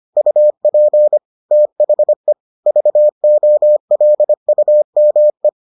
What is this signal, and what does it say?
Morse code, heard as a keyed tone, about 25 wpm, 600 Hz. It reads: UP THE VOLUME